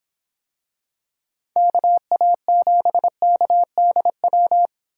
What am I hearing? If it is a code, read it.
Morse code, heard as a keyed tone, about 25 wpm, 700 Hz. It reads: KA7KDW